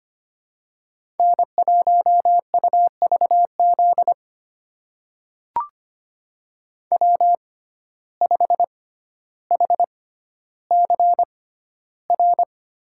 Morse code, heard as a keyed tone, about 25 wpm, spaced as for 13 wpm, 700 Hz. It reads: N1UVZ W5HCR
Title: USA Call Signs - 5 Characters - 6x Character Spacing - 25wpm